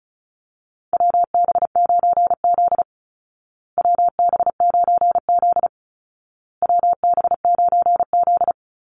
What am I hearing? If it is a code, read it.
Morse code, heard as a keyed tone, about 34 wpm, 700 Hz. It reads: WB9Z WB9Z WB9Z
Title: Call Signs Repeated 2x - 35wpm